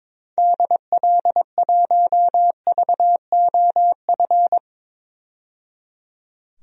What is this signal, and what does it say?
Morse code, heard as a keyed tone, about 21 wpm, 700 Hz. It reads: DL1VOF
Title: DXCC-100 Call Signs - 6 Characters - 22wpm